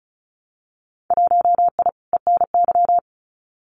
Morse code, heard as a keyed tone, about 36 wpm, 700 Hz. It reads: 1I ENY
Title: Winter Field Day Exchanges Encore 35wpm